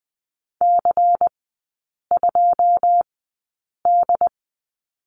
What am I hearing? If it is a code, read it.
Morse code, heard as a keyed tone, about 20 wpm, spaced as for 8 wpm, 690 Hz. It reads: C2D
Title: 093 Lesson 16e - Letter c - Three Characters 20wpm